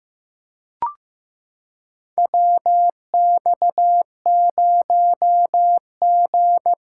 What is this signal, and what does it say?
Morse code, heard as a keyed tone, about 16 wpm, 700 Hz. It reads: WX0G
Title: USA Callsign DE Callsign - 4 Characters - 15wpm